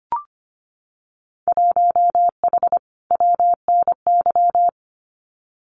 Morse code, heard as a keyed tone, about 25 wpm, 700 Hz. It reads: 1H WNY
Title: Winter Field Day Exchanges 25wpm